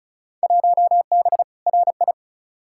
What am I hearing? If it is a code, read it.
Morse code, heard as a keyed tone, about 36 wpm, 700 Hz. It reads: 1B RI
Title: Field Day Exchanges Encore - 35wpm